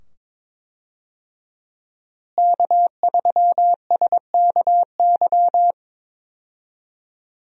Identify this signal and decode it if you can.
Morse code, heard as a keyed tone, about 22 wpm, 700 Hz. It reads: K3SKY
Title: USA Call Signs - 5 Characters - Repeated 2x - 22wpm